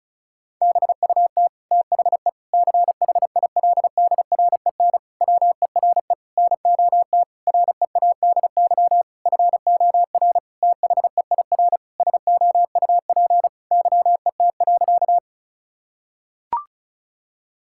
Morse code, heard as a keyed tone, about 36 wpm, 700 Hz. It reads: BUT THE CHILDREN WERE NOT READY FOR THEIR SOUP YET.
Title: Sentences from Top 1000 Words - Review Part 6 - 35wpm